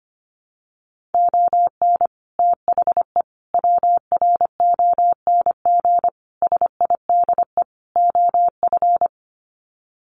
Morse code, heard as a keyed tone, about 25 wpm, 700 Hz. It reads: ON THE WRONG SIDE OF
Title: Sets of 5 Words Encore 25wpm